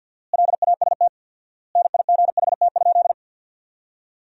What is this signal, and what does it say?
Morse code, heard as a keyed tone, about 53 wpm, 700 Hz. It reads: LAST NIGHT?
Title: 4x Word-Spacing - Sentences from Top 300 Words - 50wpm